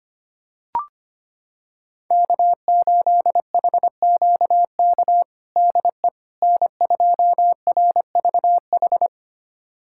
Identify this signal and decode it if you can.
Morse code, heard as a keyed tone, about 25 wpm, 700 Hz. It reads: K8HQK DE N2RVH
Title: USA Callsign DE Callsign - 5 Characters - 25wpm